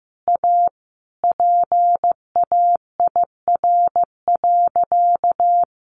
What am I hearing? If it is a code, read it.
Morse code, heard as a keyed tone, about 15 wpm, 700 Hz. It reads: A PAIR.
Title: Sentences from Top 1200 Words 15wpm